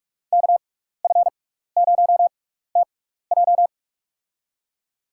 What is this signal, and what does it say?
Morse code, heard as a keyed tone, about 41 wpm, spaced as for 15 wpm, 700 Hz. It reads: KF0TJ
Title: USA Call Signs - 5 Characters - 6x Character Spacing - 45wpm